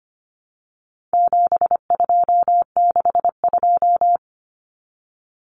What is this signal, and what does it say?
Morse code, heard as a keyed tone, about 25 wpm, 700 Hz. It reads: 7262
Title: Common Cut Numbers - 4-Digit 25wpm